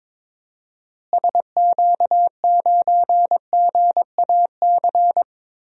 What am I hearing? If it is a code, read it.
Morse code, heard as a keyed tone, about 22 wpm, 700 Hz. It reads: SQ9GAC
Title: DXCC-100 Call Signs - 6 Characters - Repeated 3x - 22wpm